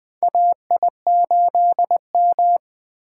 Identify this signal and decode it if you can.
Morse code, heard as a keyed tone, about 20 wpm, 700 Hz. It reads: AI8M
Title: USA Call Signs - 4 Characters - 3x Character Spacing - 20wpm